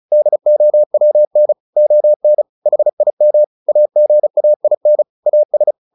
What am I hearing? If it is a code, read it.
Morse code, heard as a keyed tone, about 34 wpm, 600 Hz. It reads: DOWN ON HIM AGAIN AS
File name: the old man and the sea 2 35wpm